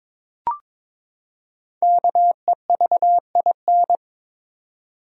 Morse code, heard as a keyed tone, about 22 wpm, 700 Hz. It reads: KEVIN
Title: USA Common Names - Max 5 Letters - 22wpm